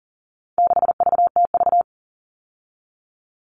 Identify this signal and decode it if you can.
Morse code, heard as a keyed tone, about 40 wpm, 700 Hz. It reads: 6VTV
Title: All Cut Numbers - 4-Digit 40wpm